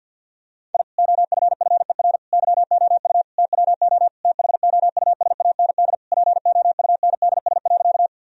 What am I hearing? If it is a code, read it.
Morse code, heard as a keyed tone, about 50 wpm, 700 Hz. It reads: I OFFER YOU TWO THOUSAND POUNDS.
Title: Sentences from Top 1000 Words - Review Part 4 - 50wpm